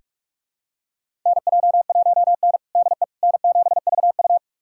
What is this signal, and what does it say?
Morse code, heard as a keyed tone, about 41 wpm, 700 Hz. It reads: NJ1N DE N7VU
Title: USA Callsign DE Callsign - 4 Characters - 45wpm